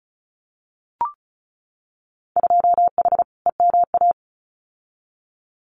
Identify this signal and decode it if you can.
Morse code, heard as a keyed tone, about 34 wpm, 700 Hz. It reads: 2H EMA
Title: Winter Field Day Exchanges 35wpm